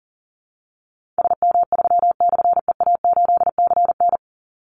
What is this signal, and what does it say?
Morse code, heard as a keyed tone, about 40 wpm, 700 Hz. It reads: SM3/EA8CN
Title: Call Signs - 40wpm